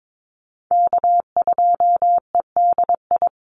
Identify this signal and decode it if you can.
Morse code, heard as a keyed tone, about 22 wpm, 700 Hz. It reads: K2EDI
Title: USA Call Signs 22wpm